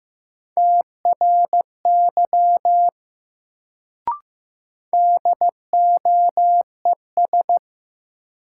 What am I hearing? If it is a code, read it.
Morse code, heard as a keyed tone, about 15 wpm, 700 Hz. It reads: TRY DOES
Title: Sentences from Top 200 Words - New Word Drill - 15wpm